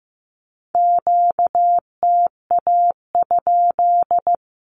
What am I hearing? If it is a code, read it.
Morse code, heard as a keyed tone, about 15 wpm, 700 Hz. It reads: QTA?
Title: Q-Codes Repeated 3x 15wpm